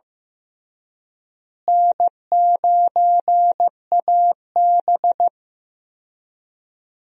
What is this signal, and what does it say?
Morse code, heard as a keyed tone, about 15 wpm, 700 Hz. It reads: N9AB
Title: USA Call Signs - 4 Characters - Repeated 2x - 15wpm